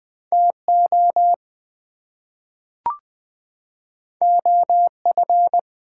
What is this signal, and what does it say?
Morse code, heard as a keyed tone, about 20 wpm, 700 Hz. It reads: TO OF
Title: Two Letter Words - 20wpm